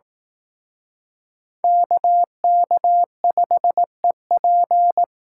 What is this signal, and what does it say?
Morse code, heard as a keyed tone, about 18 wpm, 700 Hz. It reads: KK5EP
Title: Speed Racing - USA 2x2 Call Signs 15wpm